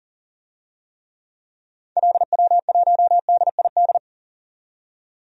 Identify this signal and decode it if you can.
Morse code, heard as a keyed tone, about 40 wpm, 700 Hz. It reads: LW1DID
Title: DXCC-100 Call Signs - 6 Characters - 40wpm